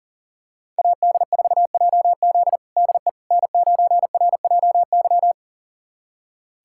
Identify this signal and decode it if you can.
Morse code, heard as a keyed tone, about 40 wpm, 700 Hz. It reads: AD4JZ DE N9RJY
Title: USA Callsign DE Callsign - 5 Characters - 40wpm